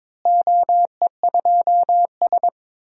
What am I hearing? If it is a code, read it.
Morse code, heard as a keyed tone, about 21 wpm, 700 Hz. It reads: OE2S
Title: DXCC-100 Call Signs - 4 Characters - Repeated 3x - 22wpm